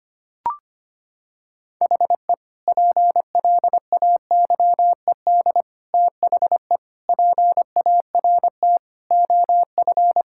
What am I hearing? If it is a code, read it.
Morse code, heard as a keyed tone, about 25 wpm, 700 Hz. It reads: HE PLAYED THE PART OF
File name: practice / 25wpm / Sentences with 4-Word Phrases - 25wpm